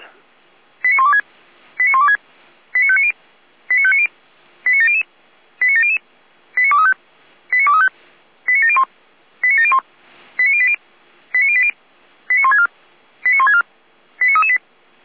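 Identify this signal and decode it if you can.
Five-tone selective call (ZVEI), heard as quick runs of five tones, tones of about 70 ms. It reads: 89126 89126 89590 89590 89700 89700 89235 89235 89701 89701 89080 89080 89164 89164 89338
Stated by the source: ZVEI five-tone selective call received by SDR.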